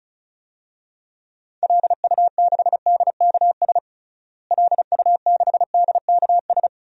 Morse code, heard as a keyed tone, about 34 wpm, 700 Hz. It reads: LU6DKS LU6DKS
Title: DXCC-100 Call Signs - 6 Characters - Repeated 3x - 35wpm